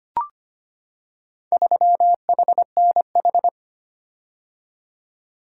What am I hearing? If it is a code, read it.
Morse code, heard as a keyed tone, about 25 wpm, 700 Hz. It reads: 3HNH
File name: practice / 25wpm / Four Letter-Number Combinations - 25wpm